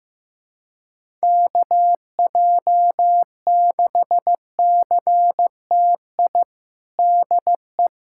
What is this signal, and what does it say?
Morse code, heard as a keyed tone, about 15 wpm, 700 Hz. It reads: KJ6CTI DE
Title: USA Callsign DE Callsign - 6 Characters - 15wpm